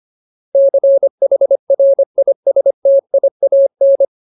Morse code, heard as a keyed tone, about 25 wpm, 550 Hz. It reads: CHRISTIAN
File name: Ultimate Word List / 25WPM Ultimate